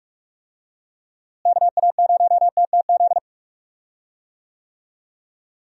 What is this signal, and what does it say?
Morse code, heard as a keyed tone, about 42 wpm, 700 Hz. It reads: KA0TTZ
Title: USA Call Signs - 6 Characters - Repeated 2x - 45wpm